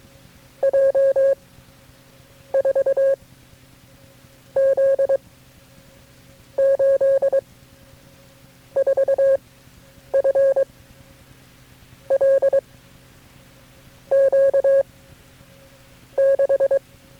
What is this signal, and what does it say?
Morse code, heard as a keyed tone, about 22 wpm, 540 Hz. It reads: J 4 Z 8 4 F L Q 6
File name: FreeFiles / CW-practice / KB6MT-tape3-side1